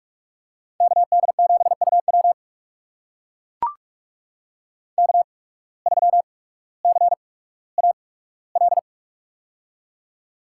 Morse code, heard as a keyed tone, about 42 wpm, spaced as for 19 wpm, 700 Hz. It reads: KD7UW K3CAL
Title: USA Call Signs - 5 Characters - 8x Character Spacing - 45wpm